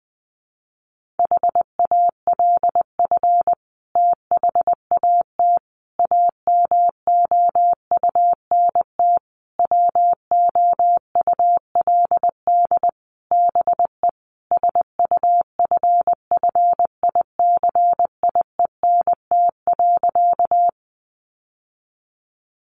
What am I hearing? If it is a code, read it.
Morse code, heard as a keyed tone, about 20 wpm, 700 Hz. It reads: HALF THAT AMOUNT WOULD BE SUFFICIENT.
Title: Sentences from Top 1400 Words 20wpm